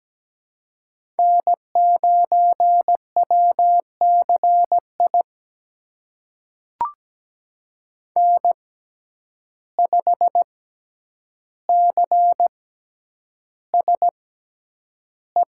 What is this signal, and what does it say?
Morse code, heard as a keyed tone, about 17 wpm, spaced as for 9 wpm, 700 Hz. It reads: N9WCI N5CSE
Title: USA Call Signs - 5 Characters - 6x Character Spacing - 17wpm